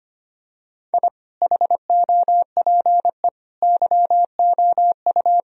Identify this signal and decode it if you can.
Morse code, heard as a keyed tone, about 25 wpm, 700 Hz. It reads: I HOPE YOU
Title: Sentences from Top 1000 Words - Review Part 3 - 25wpm